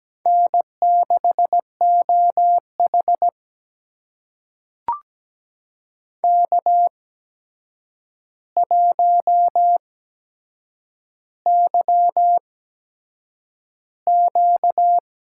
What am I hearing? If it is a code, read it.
Morse code, heard as a keyed tone, about 17 wpm, spaced as for 7 wpm, 700 Hz. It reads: N6OH K1YQ
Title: USA Call Signs - 4 Characters - 8x Character Spacing - 17wpm